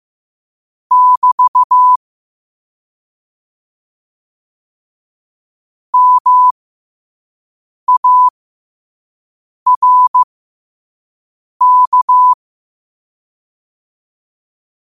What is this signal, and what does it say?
Morse code, heard as a keyed tone, about 15 wpm, spaced as for 5 wpm, 1000 Hz. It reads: <BT> MARK
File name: qotdfiles / 5wpm